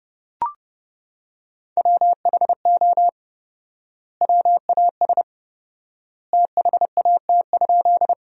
Morse code, heard as a keyed tone, about 30 wpm, 700 Hz. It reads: WHO WAS THAT?
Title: 4x Word-Spacing - Sentences from Top 100 Words - 30wpm